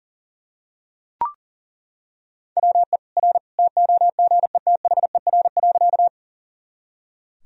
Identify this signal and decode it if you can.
Morse code, heard as a keyed tone, about 40 wpm, 700 Hz. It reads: WE R TOGETHER.
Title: Sentences from Top 300 Words and Abbreviations 40wpm